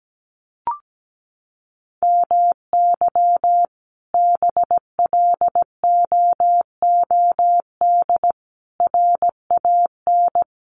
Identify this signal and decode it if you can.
Morse code, heard as a keyed tone, about 17 wpm, 700 Hz. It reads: MY BLOOD RAN
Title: Sentences from Top 1100 Words 17wpm